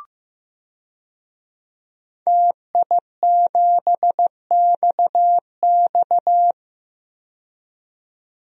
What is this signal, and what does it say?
Morse code, heard as a keyed tone, about 15 wpm, 700 Hz. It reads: TI7XX
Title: DXCC-100 Call Signs - 5 Characters - 15wpm